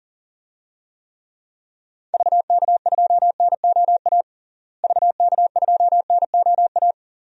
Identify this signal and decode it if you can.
Morse code, heard as a keyed tone, about 40 wpm, 700 Hz. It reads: VK2NOA VK2NOA
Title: DXCC-100 Call Signs - 6 Characters - Repeated 2x - 40wpm